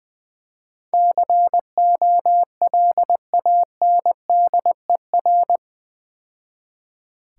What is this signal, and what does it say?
Morse code, heard as a keyed tone, about 22 wpm, 700 Hz. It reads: COLANDER
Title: Easy Words - Speed-Racing with Hints - 20wpm